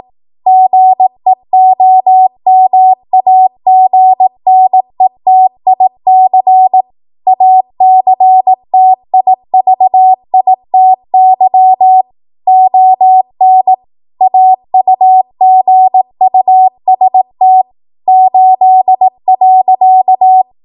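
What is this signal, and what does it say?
Morse code, heard as a keyed tone, about 18 wpm, 750 Hz. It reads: GEOMAGNETIC ACTIVITY ON AUGUST 8.
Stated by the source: ARRL 18 wpm Morse code practice archive transmission.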